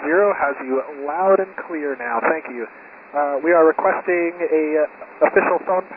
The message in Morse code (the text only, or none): none